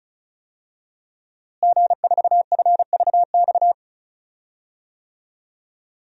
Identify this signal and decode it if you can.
Morse code, heard as a keyed tone, about 36 wpm, 700 Hz. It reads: G4FVX
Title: DXCC-100 Call Signs - 5 Characters - Repeated 2x - 35wpm